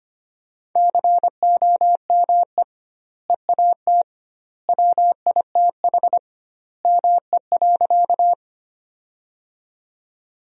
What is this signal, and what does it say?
Morse code, heard as a keyed tone, about 25 wpm, 700 Hz. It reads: COME EAT WITH ME.
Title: 2x Word-Spacing - Sentences from Top 300 Words - 25wpm